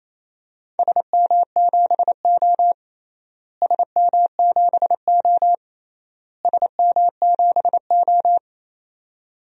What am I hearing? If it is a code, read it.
Morse code, heard as a keyed tone, about 28 wpm, 700 Hz. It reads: SM7O SM7O SM7O
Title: DXCC-100 Call Signs - 4 Characters - Repeated 3x - 28wpm